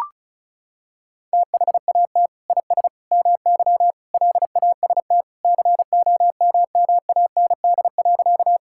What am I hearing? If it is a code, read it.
Morse code, heard as a keyed tone, about 34 wpm, 700 Hz. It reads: THAT IS MY LAST COMMAND.
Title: Sentences from Top 1600 Words 35wpm